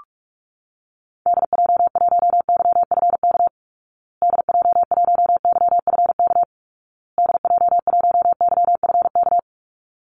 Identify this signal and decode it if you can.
Morse code, heard as a keyed tone, about 44 wpm, 700 Hz. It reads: DJ1YFK DJ1YFK DJ1YFK
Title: Call Signs Repeated 2x - 45wpm